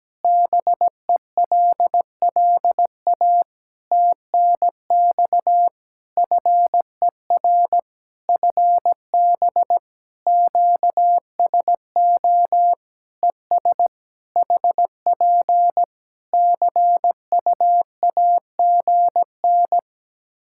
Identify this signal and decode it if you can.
Morse code, heard as a keyed tone, about 17 wpm, 700 Hz. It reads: BELLA TNX FER FB QSO ES HP CUAGN
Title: K1USN SST QSO Practice 17wpm